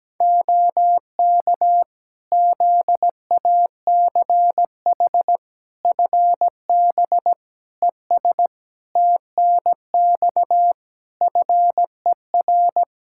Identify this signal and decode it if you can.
Morse code, heard as a keyed tone, about 17 wpm, 700 Hz. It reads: OK ZACH FB ES TNX FER
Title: K1USN SST QSO Practice 17wpm